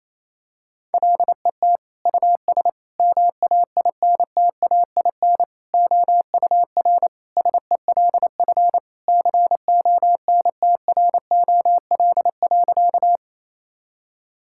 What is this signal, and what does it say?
Morse code, heard as a keyed tone, about 28 wpm, 700 Hz. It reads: LET US MAINTAIN OUR SELF CONTROL.
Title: Sentences from Top 1000 Words - Review Part 5 - 28wpm